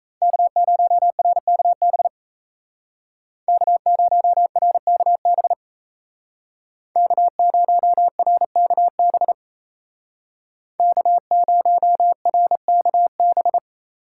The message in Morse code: K0RKB K0RKB K0RKB K0RKB

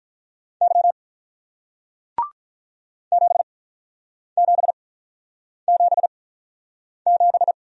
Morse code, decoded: Y 7777

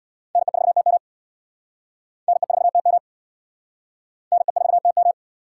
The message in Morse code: NE4TM NE4TM NE4TM

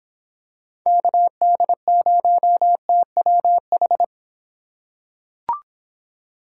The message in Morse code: KD0TWH

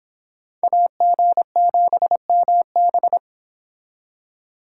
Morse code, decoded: AG7MB

